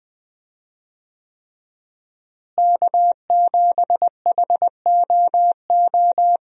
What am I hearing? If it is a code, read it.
Morse code, heard as a keyed tone, about 20 wpm, 700 Hz. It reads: K7HOO
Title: USA Call Signs - 5 Characters - Repeated 3x - 20wpm